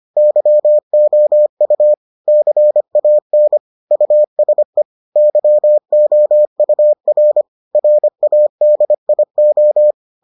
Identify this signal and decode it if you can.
Morse code, heard as a keyed tone, about 25 wpm, 600 Hz. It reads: YOU CAN USE YOUR RADIO